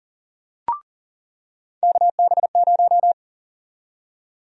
Morse code, heard as a keyed tone, about 40 wpm, 700 Hz. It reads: KB0
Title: USA Call Sign Prefixes - 40wpm